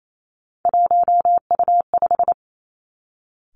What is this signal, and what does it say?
Morse code, heard as a keyed tone, about 28 wpm, 700 Hz. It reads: 1U5
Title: Three Challenging Letter-Number Combinations 28wpm